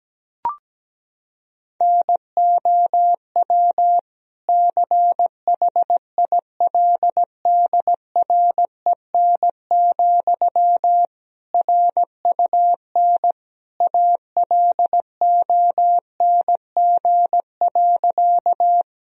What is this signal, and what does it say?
Morse code, heard as a keyed tone, about 17 wpm, 700 Hz. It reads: NOW CHILDREN, RUN ALONG.
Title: Sentences from Top 1000 Words - Review Part 1 - 17wpm